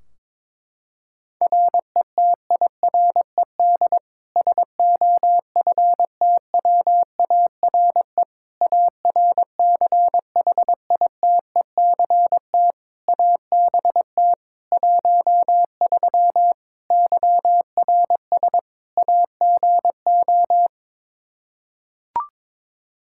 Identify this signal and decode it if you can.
Morse code, heard as a keyed tone, about 21 wpm, 700 Hz. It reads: RETIRED SOFTWARE ARCHITECT ABT 13 YRS AGO
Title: Practice QSO - Exchange 3 - More Info - 22wpm